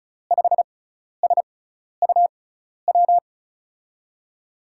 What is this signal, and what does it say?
Morse code, heard as a keyed tone, about 36 wpm, spaced as for 12 wpm, 700 Hz. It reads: 5SUW
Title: USA Call Signs - 6x Character Spacing - 35wpm